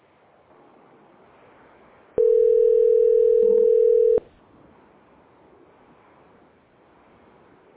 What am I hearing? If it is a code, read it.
Ringback tone (North America), 440 + 480 Hz.